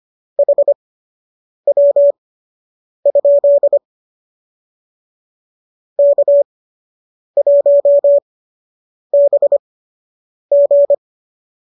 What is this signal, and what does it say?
Morse code, heard as a keyed tone, about 25 wpm, spaced as for 8 wpm, 570 Hz. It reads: HW? K1BG